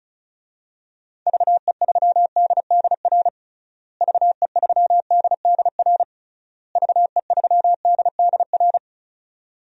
Morse code, heard as a keyed tone, about 36 wpm, 700 Hz. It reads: VE3DDR VE3DDR VE3DDR
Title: DXCC-100 Call Signs - 6 Characters - Repeated 3x - 35wpm